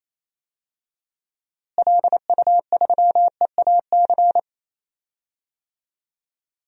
Morse code, heard as a keyed tone, about 28 wpm, 700 Hz. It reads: LU3EAC
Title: DXCC-100 Call Signs - 6 Characters - Repeated 2x - 28wpm